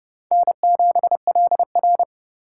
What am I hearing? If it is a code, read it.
Morse code, heard as a keyed tone, about 30 wpm, 700 Hz. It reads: N7LR
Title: K1USN SST QSO Practice - 30wpm